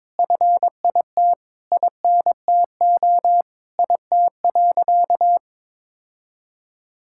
Morse code, heard as a keyed tone, about 22 wpm, 700 Hz. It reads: FIT INTO IT.
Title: Sentences from Top 1000 Words - Review Part 5 - 22wpm